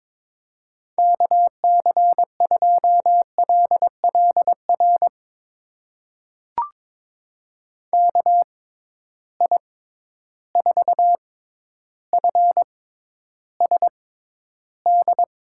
Morse code, heard as a keyed tone, about 22 wpm, spaced as for 11 wpm, 700 Hz. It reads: KC2LLR KI4FSD
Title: USA Call Signs - 6x Character Spacing - 22wpm